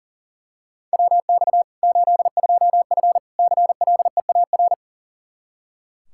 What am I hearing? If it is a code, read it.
Morse code, heard as a keyed tone, about 40 wpm, 700 Hz. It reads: WX 82F CLEAR
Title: Practice QSO - Exchange 2 - Info - 40wpm